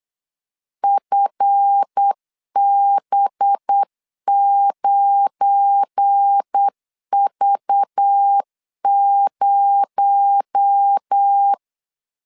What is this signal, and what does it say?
Morse code, heard as a keyed tone, about 8 wpm, 800 Hz. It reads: FB9V0
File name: LEZIONI / 7 GF9Z / 7 GF9 8wpm 3545